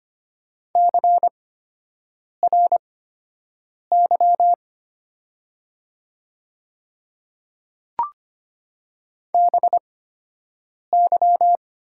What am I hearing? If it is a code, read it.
Morse code, heard as a keyed tone, about 25 wpm, spaced as for 7 wpm, 700 Hz. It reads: CRY BY
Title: 8x Character Spacing - Top 500 Words - 25wpm